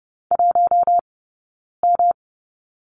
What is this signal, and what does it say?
Morse code, heard as a keyed tone, about 30 wpm, spaced as for 9 wpm, 690 Hz. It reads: 1M
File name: Lesson 18d - Letter m - Two Characters 30wpm